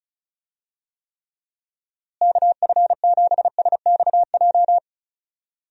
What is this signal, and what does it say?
Morse code, heard as a keyed tone, about 34 wpm, 700 Hz. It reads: KF7SXJ
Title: USA Call Signs - 6 Characters - Repeated 2x - 35wpm